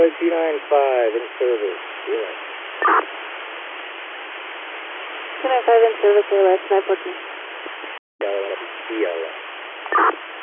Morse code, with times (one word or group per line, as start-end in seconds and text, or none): none